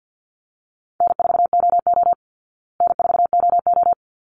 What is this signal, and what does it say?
Morse code, heard as a keyed tone, about 49 wpm, 700 Hz. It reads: N4OO N4OO
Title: Call Signs Repeated 3x - 50wpm